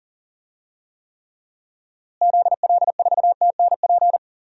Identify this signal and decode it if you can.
Morse code, heard as a keyed tone, about 40 wpm, 700 Hz. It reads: ZL4TNP
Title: DXCC-100 Call Signs - 6 Characters - Repeated 3x - 40wpm